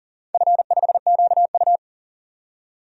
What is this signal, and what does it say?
Morse code, heard as a keyed tone, about 42 wpm, 700 Hz. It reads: F5QU
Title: DXCC-100 Call Signs - 4 Characters - 40wpm